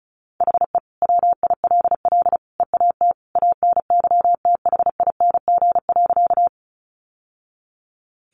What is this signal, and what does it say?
Morse code, heard as a keyed tone, about 34 wpm, 700 Hz. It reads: HE WILL EAT ANYTHING.
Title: Sentences from Top 1000 Words 35wpm